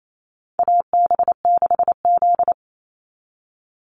Morse code, heard as a keyed tone, about 28 wpm, 700 Hz. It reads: AB6Z
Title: Call Signs Repeated 2x - 28wpm